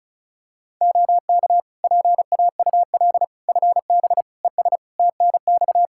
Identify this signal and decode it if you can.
Morse code, heard as a keyed tone, about 36 wpm, 700 Hz. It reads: OK PAUL FB ES TNX
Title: Practice QSO - Exchange 2 - Info - 35wpm